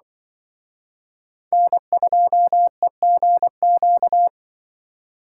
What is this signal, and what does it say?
Morse code, heard as a keyed tone, about 25 wpm, 700 Hz. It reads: N2EGQ
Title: Speed Racing - USA 1x3 Call Signs 20wpm